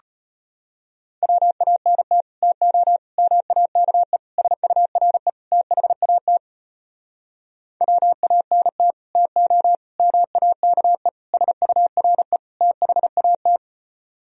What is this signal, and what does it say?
Morse code, heard as a keyed tone, about 33 wpm, 700 Hz. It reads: WANT TO MAKE SURE THAT WANT TO MAKE SURE THAT